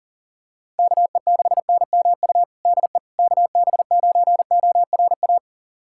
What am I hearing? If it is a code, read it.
Morse code, heard as a keyed tone, about 40 wpm, 700 Hz. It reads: KE6NMU DE KB9ORA
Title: USA Callsign DE Callsign 40wpm